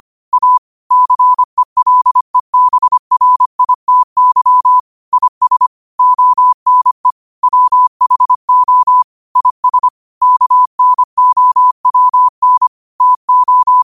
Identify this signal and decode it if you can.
Morse code, heard as a keyed tone, about 25 wpm, 1000 Hz. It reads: A CELEBRITY IS ONE WHO IS KNOWN TO
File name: qotdfiles / 25wpm